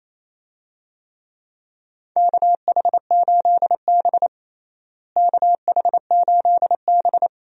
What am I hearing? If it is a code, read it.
Morse code, heard as a keyed tone, about 28 wpm, 700 Hz. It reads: KH8B KH8B
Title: USA Call Signs - 4 Characters - Repeated 2x - 28wpm